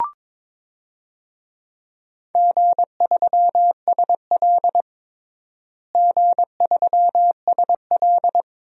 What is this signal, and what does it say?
Morse code, heard as a keyed tone, about 22 wpm, 700 Hz. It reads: G3SL G3SL
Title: DXCC-100 Call Signs - 4 Characters - Repeated 2x - 22wpm